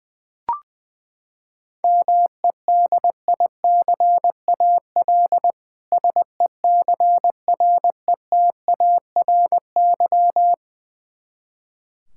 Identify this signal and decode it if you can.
Morse code, heard as a keyed tone, about 21 wpm, 700 Hz. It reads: MEDICAL SECRETARY
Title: Practice QSO - Exchange 3 - More Info - 20wpm